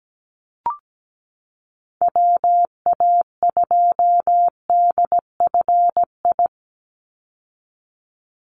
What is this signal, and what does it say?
Morse code, heard as a keyed tone, about 17 wpm, 700 Hz. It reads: WA2DFI
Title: Call Signs Repeated 2x - 17wpm